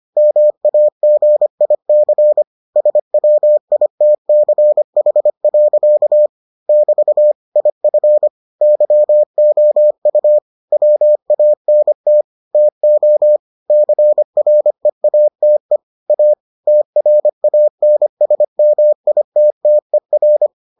Morse code, heard as a keyed tone, about 25 wpm, 600 Hz. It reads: MAGIC SWITCH. <BT> IF YOU WANT TO CREATE A TRANSMITTER